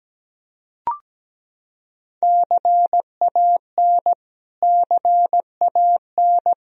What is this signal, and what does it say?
Morse code, heard as a keyed tone, about 17 wpm, 700 Hz. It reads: CAN CAN
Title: Top 100 Words Repeated 17wpm